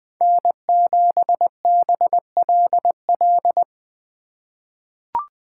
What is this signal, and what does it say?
Morse code, heard as a keyed tone, about 20 wpm, 700 Hz. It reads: N7BLL
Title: USA Call Signs - 3x Character Spacing - 20wpm